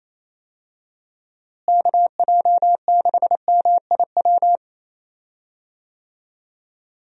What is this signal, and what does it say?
Morse code, heard as a keyed tone, about 28 wpm, 700 Hz. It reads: KJ6MIW
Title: USA Call Signs - 6 Characters - Repeated 2x - 28wpm